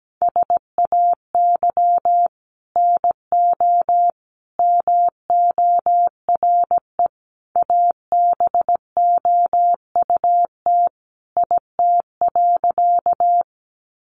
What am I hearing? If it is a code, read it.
Morse code, heard as a keyed tone, about 17 wpm, 700 Hz. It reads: SAY NO MORE ABOUT IT.
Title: Sentences from Top 200 Words 17wpm